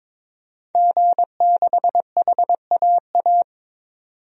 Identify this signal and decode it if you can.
Morse code, heard as a keyed tone, about 22 wpm, 700 Hz. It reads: G6HAA
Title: DXCC-100 Call Signs - 5 Characters - 22wpm